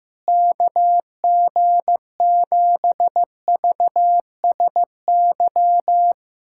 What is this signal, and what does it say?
Morse code, heard as a keyed tone, about 15 wpm, 700 Hz. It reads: KG7VSY
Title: USA Callsign DE Callsign 15wpm